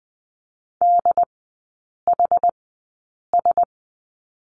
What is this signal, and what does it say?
Morse code, heard as a keyed tone, about 20 wpm, spaced as for 8 wpm, 700 Hz. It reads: DHS
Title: 074 Lesson 13e - Letter L - Three Characters 20wpm